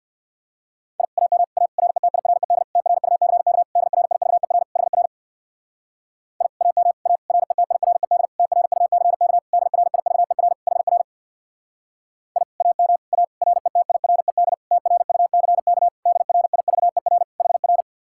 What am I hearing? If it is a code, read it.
Morse code, heard as a keyed tone, about 56 wpm, 700 Hz. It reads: I AM A RETIRED TRUCK DRIVER HR I AM A RETIRED TRUCK DRIVER HR I AM A RETIRED TRUCK DRIVER HR
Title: Speed-Racing Common QSO Phrases Encore - 50wpm